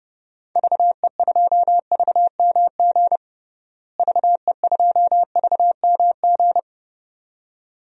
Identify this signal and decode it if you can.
Morse code, heard as a keyed tone, about 32 wpm, 700 Hz. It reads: VE2VMG VE2VMG
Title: DXCC-100 Call Signs - 6 Characters - Repeated 2x - 30wpm